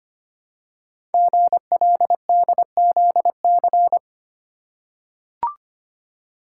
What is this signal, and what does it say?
Morse code, heard as a keyed tone, about 25 wpm, 700 Hz. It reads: GLDZC
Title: Code Groups - 5 Letters - 25wpm